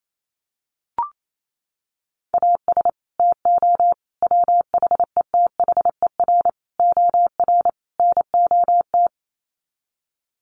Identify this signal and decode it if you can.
Morse code, heard as a keyed tone, about 28 wpm, 700 Hz. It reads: AS TO WHETHER OR NOT
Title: Sets of 5 Words Encore 28wpm